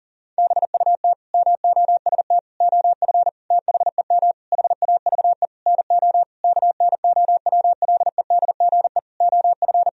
Morse code, heard as a keyed tone, about 40 wpm, 700 Hz. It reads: BUT MOST OF THEM HAVE NO KNOWLEDGE OF
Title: Sentences from Top 1000 Words - Review Part 3 - 40wpm